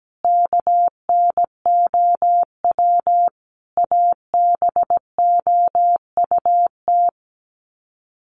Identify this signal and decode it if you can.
Morse code, heard as a keyed tone, about 17 wpm, 700 Hz. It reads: KNOW ABOUT
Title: Sets of 5 Words Encore 17wpm